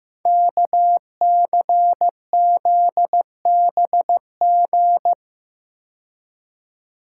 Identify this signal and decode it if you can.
Morse code, heard as a keyed tone, about 15 wpm, 700 Hz. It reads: KCZBG
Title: Code Groups - 5 Letters - 15wpm